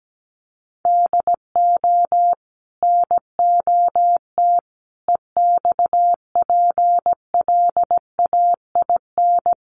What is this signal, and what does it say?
Morse code, heard as a keyed tone, about 17 wpm, 700 Hz. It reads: DO NOT EXPLAIN
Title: Sentences from Top 1100 Words 17wpm